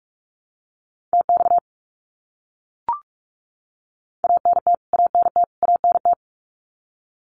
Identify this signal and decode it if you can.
Morse code, heard as a keyed tone, about 44 wpm, 700 Hz. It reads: TX ANT ANT ANT
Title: Abbreviations Repeated 3x - 45wpm